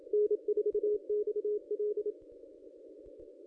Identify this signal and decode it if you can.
Morse code, heard as a keyed tone, about 27 wpm, 420 Hz. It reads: N4XL
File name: contests / ss-cw-k5tr-op-n5rz -1000